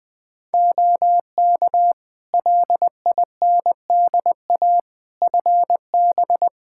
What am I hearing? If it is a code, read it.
Morse code, heard as a keyed tone, about 20 wpm, 700 Hz. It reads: OK LINDA FB
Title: Practice QSO - Exchange 2 - Info - 20wpm